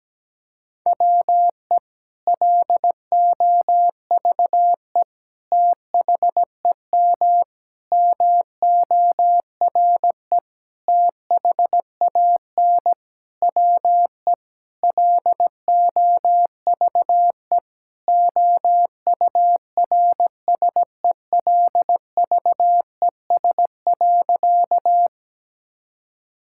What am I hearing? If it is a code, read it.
Morse code, heard as a keyed tone, about 17 wpm, 700 Hz. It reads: WE LOVE THEM MORE THAN WE LOVE OURSELVES.
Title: Sentences from Top 1600 Words 17wpm